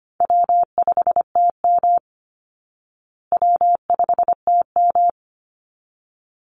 Morse code, heard as a keyed tone, about 25 wpm, 700 Hz. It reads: W5TM W5TM
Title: Call Signs Repeated 3x - 25wpm